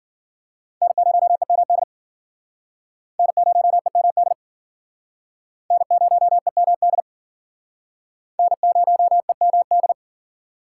Morse code, heard as a keyed tone, about 52 wpm, 700 Hz. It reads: N0EMD N0EMD N0EMD N0EMD